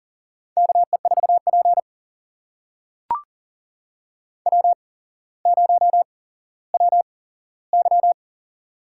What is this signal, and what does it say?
Morse code, heard as a keyed tone, about 39 wpm, spaced as for 17 wpm, 700 Hz. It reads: KE4P W0WY
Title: USA Call Signs - 4 Characters - 8x Character Spacing - 40wpm